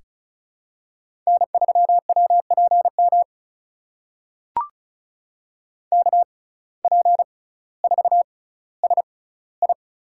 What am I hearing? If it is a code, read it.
Morse code, heard as a keyed tone, about 34 wpm, spaced as for 18 wpm, 700 Hz. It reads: N3WPM KP4SI
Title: USA Call Signs - 5 Characters - 6x Character Spacing - 35wpm